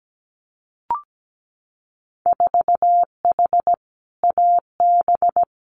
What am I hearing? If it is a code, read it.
Morse code, heard as a keyed tone, about 17 wpm, 700 Hz. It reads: 4H AB
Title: Winter Field Day Exchanges Encore 17wpm